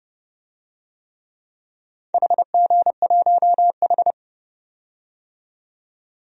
Morse code, heard as a keyed tone, about 30 wpm, 700 Hz. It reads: HG1H
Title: DXCC-100 Call Signs - 4 Characters - 30wpm